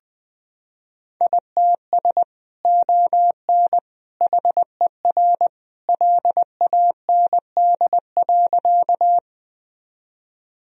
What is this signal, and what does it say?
Morse code, heard as a keyed tone, about 20 wpm, 700 Hz. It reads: ITS ON HER LAND.
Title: Sentences from Top 1000 Words - Review Part 1 - 20wpm